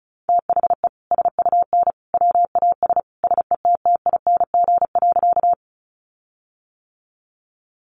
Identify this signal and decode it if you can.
Morse code, heard as a keyed tone, about 34 wpm, 700 Hz. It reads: THE SUN WAS SETTING.
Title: Sentences from Top 1300 Words 35wpm